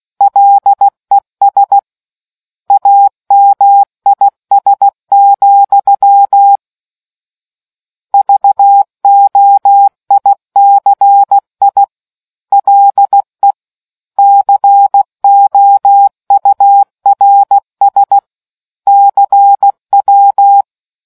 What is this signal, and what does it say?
Morse code, heard as a keyed tone, about 16 wpm, 800 Hz. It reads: LES AMIS, VOICI LE COURS CW